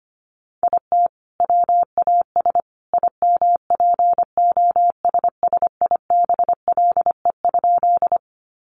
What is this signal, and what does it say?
Morse code, heard as a keyed tone, about 25 wpm, 690 Hz. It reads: IT WAS IMPOSSIBLE?
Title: Sentences from Top 1100 Words 25wpm